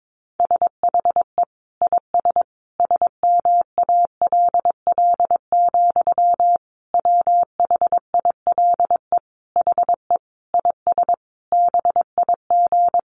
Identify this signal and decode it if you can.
Morse code, heard as a keyed tone, about 22 wpm, 700 Hz. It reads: SHE IS SMALL, WHILE HE IS BIG
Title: Sentences from Top 1300 Words 22wpm